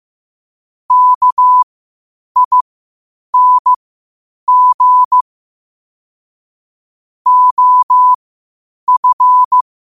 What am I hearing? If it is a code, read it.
Morse code, heard as a keyed tone, about 15 wpm, spaced as for 8 wpm, 1000 Hz. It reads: KING OF
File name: qotdfiles / 8wpm